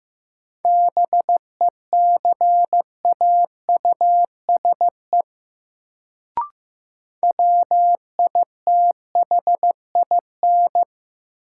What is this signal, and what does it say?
Morse code, heard as a keyed tone, about 15 wpm, 700 Hz. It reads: BECAUSE WITHIN
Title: Sets of 2 Words - New Word Drill - 15wpm